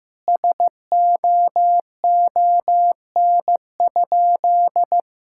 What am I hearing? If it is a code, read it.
Morse code, heard as a keyed tone, about 15 wpm, 700 Hz. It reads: SOON?
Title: Sentences from Top 1600 Words 15wpm